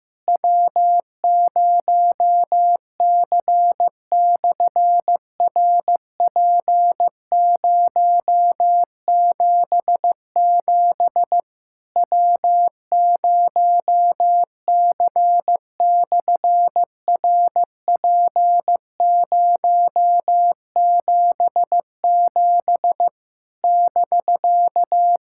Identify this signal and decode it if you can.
Morse code, heard as a keyed tone, about 15 wpm, 700 Hz. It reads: W0C/RP077 W0C/RP077 <BK>
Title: SOTA QSO Practice 15wpm